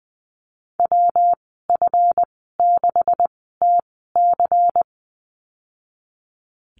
Morse code, heard as a keyed tone, about 20 wpm, spaced as for 15 wpm, 700 Hz. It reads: WF6TC
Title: 147 Lesson 24g - Letter y - Call Signs 20wpm